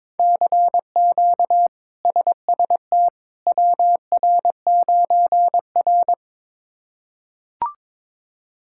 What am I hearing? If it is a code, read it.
Morse code, heard as a keyed tone, about 21 wpm, 700 Hz. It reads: CQ SST WR9R
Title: K1USN SST QSO Practice - 22wpm